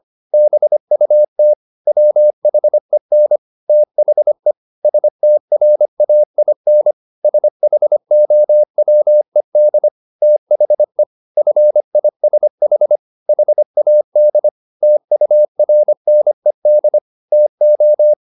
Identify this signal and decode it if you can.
Morse code, heard as a keyed tone, about 25 wpm, 600 Hz. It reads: BUT WHEN THE STRAIN SHOWED THE FISH HAD TURNED TO